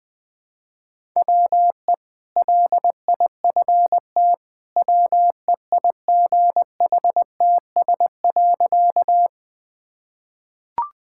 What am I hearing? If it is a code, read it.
Morse code, heard as a keyed tone, about 20 wpm, 700 Hz. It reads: WE LIFT WEIGHTS.
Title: Easy Sentences with Hints - 20wpm